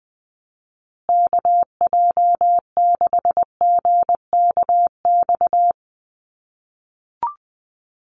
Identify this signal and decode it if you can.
Morse code, heard as a keyed tone, about 20 wpm, 700 Hz. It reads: KJ6GKX
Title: USA Call Signs Repeated 3x 20wpm